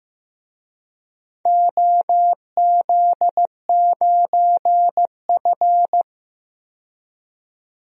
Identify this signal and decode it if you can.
Morse code, heard as a keyed tone, about 15 wpm, 700 Hz. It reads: OZ9F
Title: DXCC-100 Call Signs - 4 Characters - 15wpm